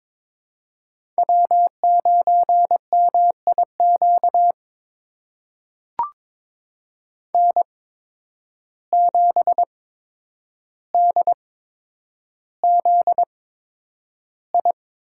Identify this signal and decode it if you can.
Morse code, heard as a keyed tone, about 22 wpm, spaced as for 9 wpm, 700 Hz. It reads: W9MIQ N7DZI
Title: USA Call Signs - 5 Characters - 8x Character Spacing - 22wpm